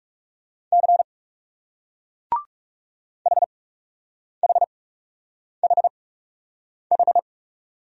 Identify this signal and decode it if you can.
Morse code, heard as a keyed tone, about 38 wpm, spaced as for 8 wpm, 700 Hz. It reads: C HHHH